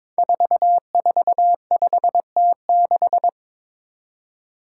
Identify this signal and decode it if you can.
Morse code, heard as a keyed tone, about 22 wpm, 700 Hz. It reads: 445T6
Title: Five Challenging Letter-Number Combinations 22wpm